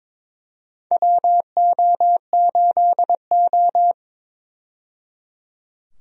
Morse code, heard as a keyed tone, about 22 wpm, 700 Hz. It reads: WO8O
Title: USA Call Signs - 4 Characters - 22wpm